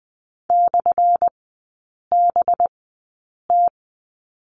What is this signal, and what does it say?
Morse code, heard as a keyed tone, about 20 wpm, spaced as for 8 wpm, 700 Hz. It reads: /BT
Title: 184 Lesson 30e - Letter b - Three Characters 20wpm